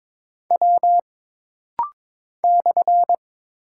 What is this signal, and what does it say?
Morse code, heard as a keyed tone, about 22 wpm, 700 Hz. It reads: W /
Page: This rapid-fire timing is geared to push your ICR (Instant Character Recognition) skills.